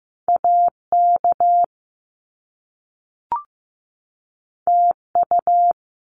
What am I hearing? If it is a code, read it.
Morse code, heard as a keyed tone, about 15 wpm, 700 Hz. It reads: AK TU
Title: Top Two Letter Combinations in Words 15wpm